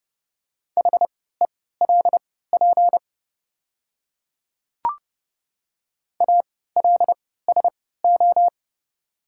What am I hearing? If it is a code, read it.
Morse code, heard as a keyed tone, about 30 wpm, spaced as for 17 wpm, 700 Hz. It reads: HELP ALSO
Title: 3x Character Spacing - Top 200 Words - 30wpm